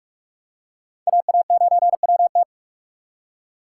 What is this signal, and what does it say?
Morse code, heard as a keyed tone, about 44 wpm, 700 Hz. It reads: AA9WT